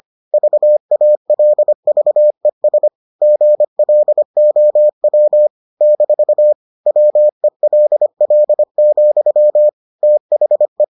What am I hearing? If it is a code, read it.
Morse code, heard as a keyed tone, about 25 wpm, 600 Hz. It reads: VALVES GLOW - WELL, THE